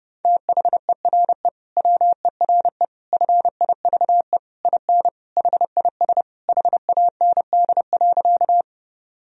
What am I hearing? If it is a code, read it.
Morse code, heard as a keyed tone, about 30 wpm, 700 Hz. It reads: THERE WERE FIVE IN HIS HAND.
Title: Sentences from Top 1000 Words - Review Part 2 - 30wpm